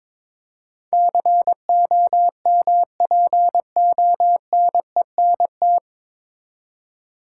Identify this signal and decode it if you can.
Morse code, heard as a keyed tone, about 21 wpm, 700 Hz. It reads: COMPONENT
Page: Morse Code Ninja practice 22wpm